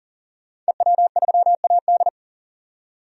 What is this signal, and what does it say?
Morse code, heard as a keyed tone, about 37 wpm, 700 Hz. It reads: EW3AD
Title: DXCC-100 Call Signs - 5 Characters - 40wpm